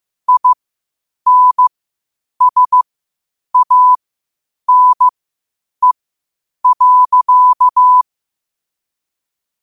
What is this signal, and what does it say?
Morse code, heard as a keyed tone, about 15 wpm, spaced as for 8 wpm, 1000 Hz. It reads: INSANE.
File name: qotdfiles / 8wpm